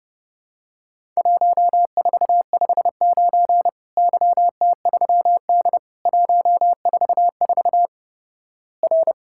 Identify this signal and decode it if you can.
Morse code, heard as a keyed tone, about 32 wpm, 700 Hz. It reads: 1459 YT3D 144 R
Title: Worked All Europe DX Contest - QTC Exchange - 30wpm